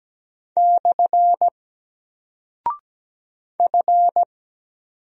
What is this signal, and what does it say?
Morse code, heard as a keyed tone, about 17 wpm, 700 Hz. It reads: / F
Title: Single Character (Letter:Number:Punctuation:Prosign) Alt 17wpm